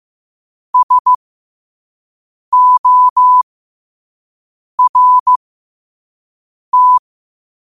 Morse code, heard as a keyed tone, about 15 wpm, spaced as for 5 wpm, 1000 Hz. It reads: SORT
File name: qotdfiles / 5wpm